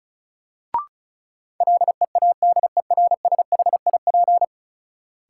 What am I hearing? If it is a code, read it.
Morse code, heard as a keyed tone, about 36 wpm, 700 Hz. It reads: LEADERSHIP
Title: Sentences from Top 1600 Words - New Word Drill - 35wpm